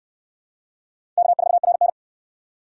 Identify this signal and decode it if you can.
Morse code, heard as a keyed tone, about 68 wpm, 700 Hz. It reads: K4RN